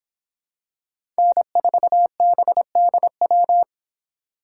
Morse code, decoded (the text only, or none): N4BDW